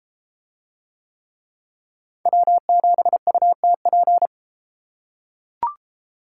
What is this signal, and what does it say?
Morse code, heard as a keyed tone, about 31 wpm, 700 Hz. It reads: W7UTP